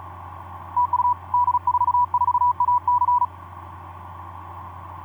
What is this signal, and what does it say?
Morse code, heard as a keyed tone, about 37 wpm, 980 Hz. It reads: TU D44AC
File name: d44ac ek2rgt etc 10m cw 23nov2010 1202